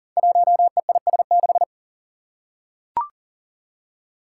Morse code, heard as a keyed tone, about 40 wpm, 700 Hz. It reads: 1EIS6
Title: Five Challenging Letter-Number Combinations 40wpm